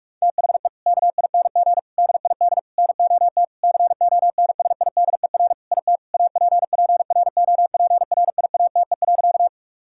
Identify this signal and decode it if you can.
Morse code, heard as a keyed tone, about 41 wpm, 700 Hz. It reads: THE KING DID NOT CONSIDER IT APPROPRIATE.
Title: Sentences from Top 1000 Words - Review Part 5 - 45wpm